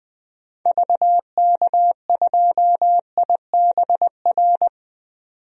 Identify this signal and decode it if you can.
Morse code, heard as a keyed tone, about 20 wpm, 700 Hz. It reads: VK2IBR
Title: DXCC-100 Call Signs - 6 Characters - 20wpm